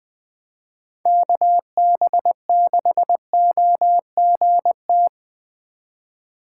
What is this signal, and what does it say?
Morse code, heard as a keyed tone, about 20 wpm, 700 Hz. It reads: KB6OGT